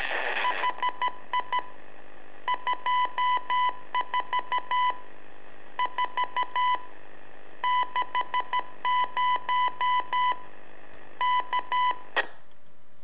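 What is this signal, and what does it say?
Morse code, heard as a keyed tone, about 13 wpm, 1000 Hz. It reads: HI 24 4 60 K
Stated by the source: Telemetry in Morse from LU7AA and LU4EZ, AMSAT and Turma Mar del Plata, sent from a balloon.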